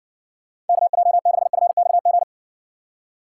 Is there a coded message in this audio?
Yes: Morse code from 0.7 to 2.2 s, heard as a keyed tone, about 60 wpm, 700 Hz. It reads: KJ6FXG